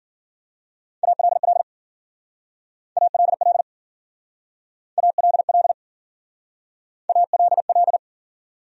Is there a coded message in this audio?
Yes: Morse code from 1.0 to 8.0 s, heard as a keyed tone, about 52 wpm, 700 Hz. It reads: ALL ALL ALL ALL